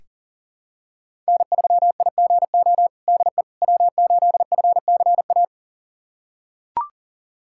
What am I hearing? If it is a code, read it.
Morse code, heard as a keyed tone, about 40 wpm, 700 Hz. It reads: N3IGO DE W8FCA
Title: USA Callsign DE Callsign - 5 Characters - 40wpm